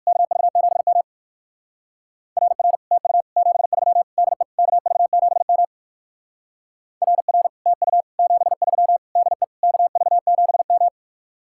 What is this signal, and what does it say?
Morse code, heard as a keyed tone, about 65 wpm, spaced as for 58 wpm, 700 Hz. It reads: KV7M RR TU 73 DE KV7M RR TU 73 DE KV7M